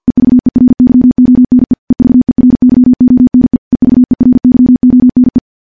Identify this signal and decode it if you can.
Morse code, heard as a keyed tone, about 50 wpm, 250 Hz. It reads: EVERYONE EVERYONE EVERYONE
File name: morse / morse-training-50WPM